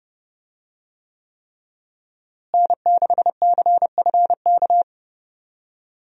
Morse code, heard as a keyed tone, about 30 wpm, 700 Hz. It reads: N6CFK
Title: USA Call Signs - 5 Characters - Repeated 3x - 30wpm